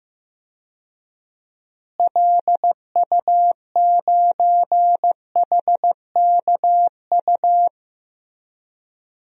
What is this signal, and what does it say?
Morse code, heard as a keyed tone, about 15 wpm, 700 Hz. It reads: LU9HKU
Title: DXCC-100 Call Signs - 6 Characters - Repeated 3x - 15wpm